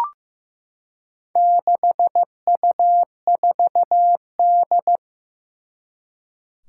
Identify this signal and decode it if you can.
Morse code, heard as a keyed tone, about 15 wpm, 700 Hz. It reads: 6U4D
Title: Four Challenging Letter-Number Combinations 15wpm